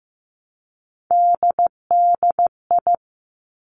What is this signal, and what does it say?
Morse code, heard as a keyed tone, about 15 wpm, 700 Hz. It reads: DDI
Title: Three Challenging Letter-Number Combinations 15wpm